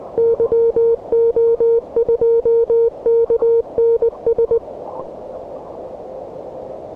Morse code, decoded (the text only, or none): YO2KNS